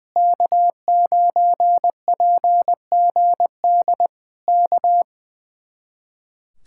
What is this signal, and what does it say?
Morse code, heard as a keyed tone, about 20 wpm, 700 Hz. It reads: K9PGD K
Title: Speed-Racing Common QSO Phrases Encore - 20wpm